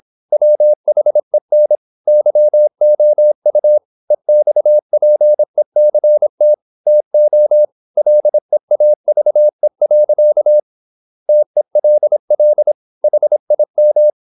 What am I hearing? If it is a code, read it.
Morse code, heard as a keyed tone, about 26 wpm, 600 Hz. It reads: WHEN YOU EXPECT TO LEAVE. TELL HIM
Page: Never 26wpm | CW med Gnister